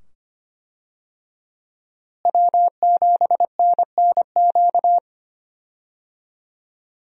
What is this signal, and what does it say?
Morse code, heard as a keyed tone, about 25 wpm, 700 Hz. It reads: W7NNQ
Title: USA Call Signs - 5 Characters - Repeated 2x - 25wpm